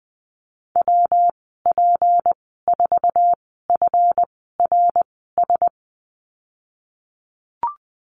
Morse code, heard as a keyed tone, about 20 wpm, spaced as for 15 wpm, 700 Hz. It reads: WP4FRS
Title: 154 Lesson 25g - Letter p - Call Signs 20wpm